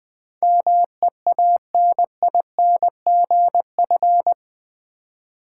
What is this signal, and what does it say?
Morse code, heard as a keyed tone, about 20 wpm, 700 Hz. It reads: MEANINGF
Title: Build A Word - 4 Letters At A Time - 10-Letter Words - 20wpm